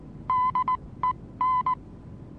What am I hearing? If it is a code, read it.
Morse code, heard as a keyed tone, about 19 wpm, spaced as for 16 wpm, 1010 Hz. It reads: DEN
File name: repos / Morsekurs / DJ4UF Morsen26